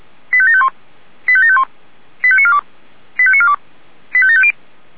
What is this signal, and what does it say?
Five-tone selective call (ZVEI), heard as quick runs of five tones, tones of about 70 ms. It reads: 85751 85751 85942 85942 85750
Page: /464/SDR/ZVEI/85730/